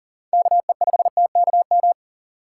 Morse code, heard as a keyed tone, about 40 wpm, 700 Hz. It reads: KE5TKM
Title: USA Call Signs - 6 Characters - 4x Character Spacing - 40wpm